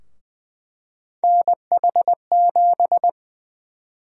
Morse code, heard as a keyed tone, about 20 wpm, 700 Hz. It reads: NH7
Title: USA Call Sign Prefixes - 20wpm